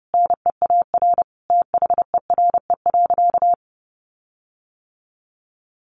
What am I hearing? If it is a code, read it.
Morse code, heard as a keyed tone, about 28 wpm, 700 Hz. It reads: NEAR THERE.
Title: Sentences from Top 1400 Words 30wpm